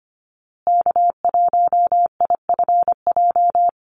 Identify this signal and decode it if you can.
Morse code, heard as a keyed tone, about 25 wpm, 700 Hz. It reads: K1IFJ
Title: Call Signs Repeated 3x - 25wpm